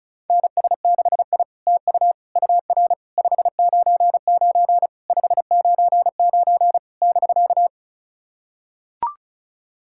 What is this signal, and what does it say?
Morse code, heard as a keyed tone, about 36 wpm, 690 Hz. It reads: NS6I TU UR 599 599 <BK>
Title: SOTA QSO Practice 35wpm